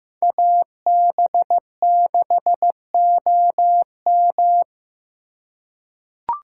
Morse code, heard as a keyed tone, about 15 wpm, 700 Hz. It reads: AB6OM
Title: USA Call Signs - 8x Character Spacing - 15wpm